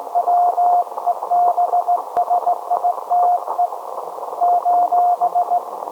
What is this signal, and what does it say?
Morse code, heard as a keyed tone, about 18 wpm, 710 Hz, with radio noise. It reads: WEBSITE :
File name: storage / message Grimeton SAQ